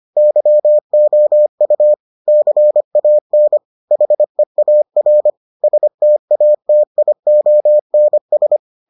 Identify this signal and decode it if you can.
Morse code, heard as a keyed tone, about 25 wpm, 600 Hz. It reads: YOU CAN HEAR STATIONS